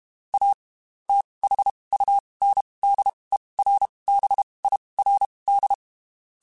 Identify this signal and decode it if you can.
Morse code, heard as a keyed tone, about 32 wpm, spaced as for 23 wpm, 800 Hz. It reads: A THUNDERBIRD
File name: QSOs / QSOs-mp3s / WD0BGP 1@30